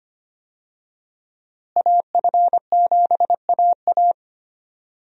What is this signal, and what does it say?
Morse code, heard as a keyed tone, about 25 wpm, 700 Hz. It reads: AF7AA
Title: USA Call Signs - 5 Characters - Repeated 3x - 25wpm